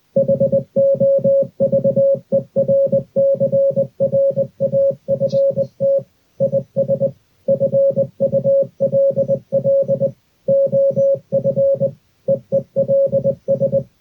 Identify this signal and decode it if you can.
Morse code, heard as a keyed tone, about 20 wpm, 550 Hz. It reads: HOVERCRAFT IS FULL OF EELS
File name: hovercraft morsecode